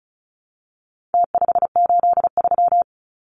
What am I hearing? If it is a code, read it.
Morse code, heard as a keyed tone, about 34 wpm, 700 Hz. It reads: T583
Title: Numbers - Contest Style - 4-Digit 35wpm